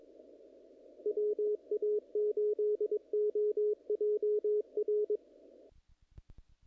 Morse code, heard as a keyed tone, about 24 wpm, 420 Hz. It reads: WA8OJR
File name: contests / naqp-cw-k5tr-op-k5pi- -1800